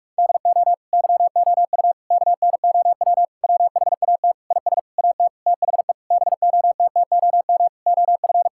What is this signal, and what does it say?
Morse code, heard as a keyed tone, about 46 wpm, 700 Hz. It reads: DO YOU KNOW WHAT IS AT THE BOTTOM OF
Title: Sentences from Top 1000 Words - Review Part 6 - 45wpm